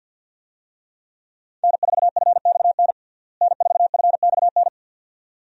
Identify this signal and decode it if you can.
Morse code, heard as a keyed tone, about 50 wpm, 700 Hz. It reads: N4FXN N4FXN
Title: USA Call Signs - 5 Characters - Repeated 2x - 50wpm